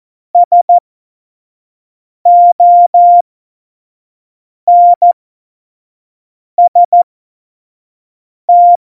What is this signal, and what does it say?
Morse code, heard as a keyed tone, about 14 wpm, spaced as for 5 wpm, 700 Hz. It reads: SONST
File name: Morsekurs / Witze / Witz 08. August 17, Tempo 25 F70